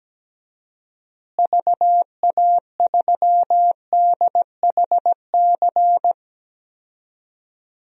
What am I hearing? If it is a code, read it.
Morse code, heard as a keyed tone, about 17 wpm, 700 Hz. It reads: VA3DHC
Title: DXCC-100 Call Signs - 6 Characters - Repeated 2x - 17wpm